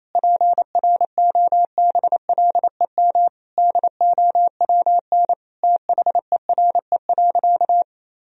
Morse code, heard as a keyed tone, about 28 wpm, 700 Hz. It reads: PROBLEM DOWN THERE.
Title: Sentences from Top 1000 Words - Review Part 4 - 28wpm